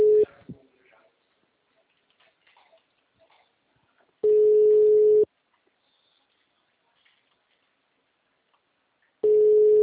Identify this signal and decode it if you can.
Ringback tone (Europe, 425 Hz).